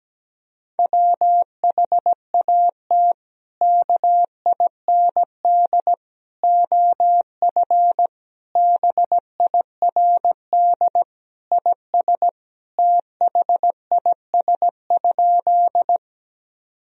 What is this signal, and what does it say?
Morse code, heard as a keyed tone, about 17 wpm, 700 Hz. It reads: WHAT KIND OF BIRD IS THIS?
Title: Sentences from Top 1000 Words - Review Part 2 - 17wpm